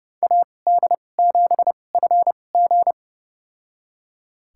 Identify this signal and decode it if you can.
Morse code, heard as a keyed tone, about 30 wpm, spaced as for 22 wpm, 700 Hz. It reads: AD7FG
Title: USA Call Signs - 2x Character Spacing - 30wpm